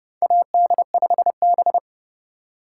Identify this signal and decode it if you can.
Morse code, heard as a keyed tone, about 30 wpm, 700 Hz. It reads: AD5B
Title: USA Call Signs - 4 Characters - 4x Character Spacing - 30wpm